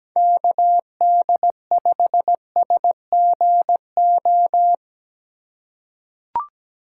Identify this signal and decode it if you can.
Morse code, heard as a keyed tone, about 17 wpm, 700 Hz. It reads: KD5SGO
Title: USA Call Signs - 6 Characters - 2x Character Spacing - 17wpm